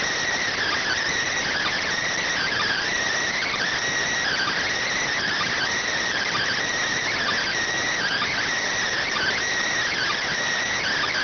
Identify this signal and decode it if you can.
Emergency Alert System (SAME) tones (header bursts), heard as harsh buzzing bursts. It is heard from 0.6 to 10.3 s.